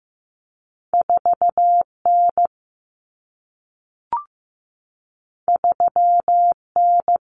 Morse code, heard as a keyed tone, about 15 wpm, 700 Hz. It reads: 4 N 3 N
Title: Common Cut Numbers - 2-Digit 15wpm